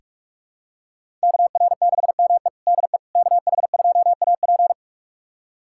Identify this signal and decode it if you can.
Morse code, heard as a keyed tone, about 46 wpm, 700 Hz. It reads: KR6ME DE KH2AP
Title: USA Callsign DE Callsign - 5 Characters - 45wpm